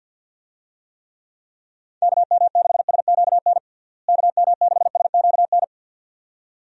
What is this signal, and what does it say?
Morse code, heard as a keyed tone, about 53 wpm, 700 Hz. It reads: KM6SQN KM6SQN
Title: USA Call Signs - 6 Characters - Repeated 2x - 50wpm